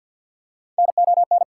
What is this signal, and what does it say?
Morse code, heard as a keyed tone, about 54 wpm, 690 Hz. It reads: NON
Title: Prefixes and Suffixes 50wpm